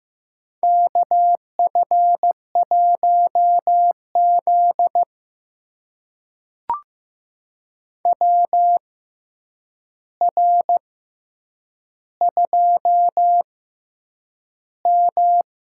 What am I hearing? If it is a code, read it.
Morse code, heard as a keyed tone, about 15 wpm, spaced as for 8 wpm, 700 Hz. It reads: KF1Z WR2M
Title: USA Call Signs - 4 Characters - 6x Character Spacing - 15wpm